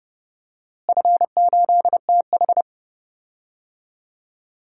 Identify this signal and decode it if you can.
Morse code, heard as a keyed tone, about 32 wpm, 700 Hz. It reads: F8TH
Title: DXCC-100 Call Signs - 4 Characters - Repeated 2x - 30wpm